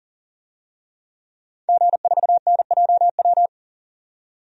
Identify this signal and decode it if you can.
Morse code, heard as a keyed tone, about 40 wpm, 700 Hz. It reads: G4NJW
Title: DXCC-100 Call Signs - 5 Characters - 40wpm